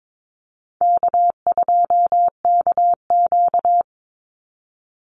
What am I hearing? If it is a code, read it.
Morse code, heard as a keyed tone, about 22 wpm, 700 Hz. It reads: K2KQ
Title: Easy Call Signs - 22wpm